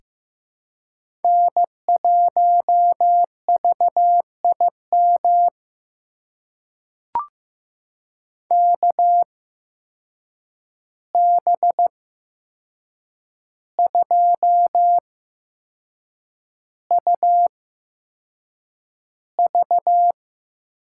Morse code, heard as a keyed tone, about 15 wpm, spaced as for 6 wpm, 700 Hz. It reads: N1VIM KB2UV
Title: USA Call Signs - 8x Character Spacing - 15wpm